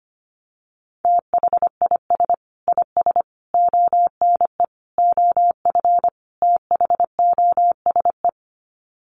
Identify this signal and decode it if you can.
Morse code, heard as a keyed tone, about 25 wpm, 700 Hz. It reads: THIS IS ONE OF THOSE
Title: Sets of 5 Words Encore2 25wpm